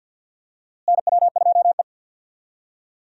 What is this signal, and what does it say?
Morse code, heard as a keyed tone, about 48 wpm, 700 Hz. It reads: NW2E
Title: USA Call Signs - 4 Characters - Repeated 3x - 50wpm